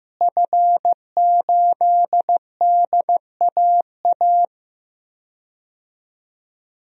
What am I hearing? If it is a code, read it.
Morse code, heard as a keyed tone, about 15 wpm, 700 Hz. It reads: F8DAA
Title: DXCC-100 Call Signs - 5 Characters - Repeated 3x - 15wpm